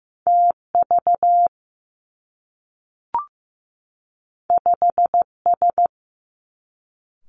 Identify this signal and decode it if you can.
Morse code, heard as a keyed tone, about 15 wpm, 700 Hz. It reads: TV 5S
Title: Two Challenging Letter-Number Combinations 15wpm